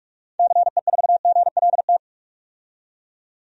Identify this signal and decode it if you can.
Morse code, heard as a keyed tone, about 43 wpm, 700 Hz. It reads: CE4GLT
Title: DXCC-100 Call Signs - 6 Characters - 45wpm